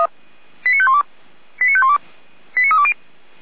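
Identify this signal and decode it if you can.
Five-tone selective call (ZVEI), heard as quick runs of five tones, tones of about 70 ms. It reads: 89512 89512 89320
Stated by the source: ZVEI five-tone selective call 89023 received on an SDR.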